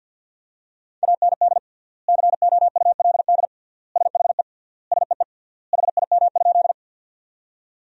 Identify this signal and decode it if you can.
Morse code, heard as a keyed tone, about 50 wpm, 700 Hz. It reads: AND COULD SHE SEE HIM?
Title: 3x Word-Spacing - Sentences from Top 100 Words - 50wpm